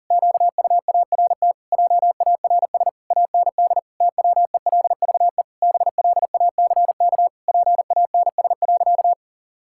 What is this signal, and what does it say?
Morse code, heard as a keyed tone, about 40 wpm, 700 Hz. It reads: QUART JARS AND TWELVE BLACK PANS.
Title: Pangram Challenge v2 - 40wpm